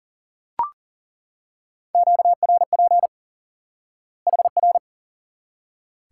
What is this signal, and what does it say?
Morse code, heard as a keyed tone, about 40 wpm, 700 Hz. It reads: QRP HR
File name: IQE Course 3 - Lesson 89 - YAGI - Two Elements - 40wpm